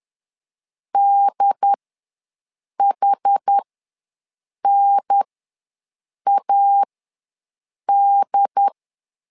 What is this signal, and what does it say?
Morse code, heard as a keyed tone, about 10 wpm, spaced as for 6 wpm, 800 Hz. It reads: DHNAD
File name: 2 UHDN ws 10wpm